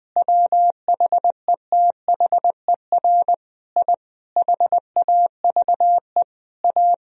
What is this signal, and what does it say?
Morse code, heard as a keyed tone, about 21 wpm, 700 Hz. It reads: WHETHER I HAVE A
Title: Sentences from Top 1000 Words - Review Part 3 - 20wpm